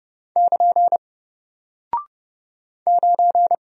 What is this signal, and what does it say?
Morse code, heard as a keyed tone, about 30 wpm, 700 Hz. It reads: <KN> 9
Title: Single Character (Letter:Number:Punctuation:Prosign) Alt 30wpm